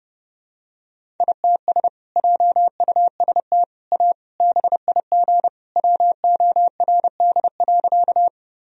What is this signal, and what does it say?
Morse code, heard as a keyed tone, about 32 wpm, 700 Hz. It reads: ITS JUST A BIG WORD.
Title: Sentences from Top 1000 Words - Review Part 1 - 30wpm